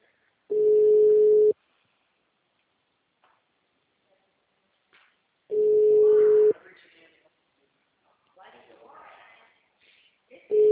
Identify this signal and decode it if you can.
Ringback tone (Europe, 425 Hz), with speech.